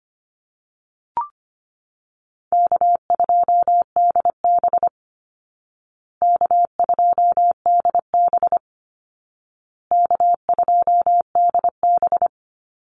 Morse code, heard as a keyed tone, about 25 wpm, 700 Hz. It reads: K2DB K2DB K2DB
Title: Easy Call Signs Repeated 3x - 25wpm